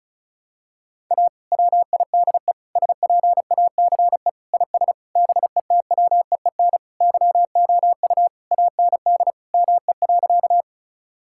Morse code, heard as a keyed tone, about 36 wpm, 700 Hz. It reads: A WIDE SPACE IS BETWEEN YOU AND ME.
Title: Sentences from Top 1000 Words - Review Part 6 - 35wpm